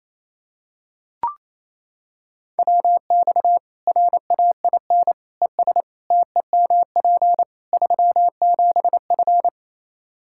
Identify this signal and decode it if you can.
Morse code, heard as a keyed tone, about 28 wpm, 700 Hz. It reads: WX RAIN ES TEMP 37F
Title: Practice QSO - Exchange 2 - Info - 28wpm